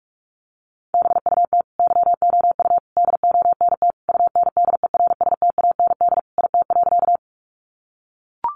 Morse code, heard as a keyed tone, about 44 wpm, 700 Hz. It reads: BUT YOU DONT UNDERSTAND IT.
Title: Sentences from Top 600 Words 45wpm